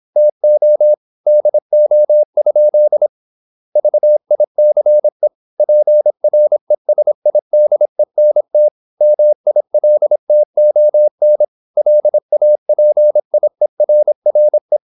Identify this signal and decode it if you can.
Morse code, heard as a keyed tone, about 26 wpm, 600 Hz. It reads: TO DO? VICE PRESIDENT MILTON LAPIERRE
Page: Never 26wpm | CW med Gnister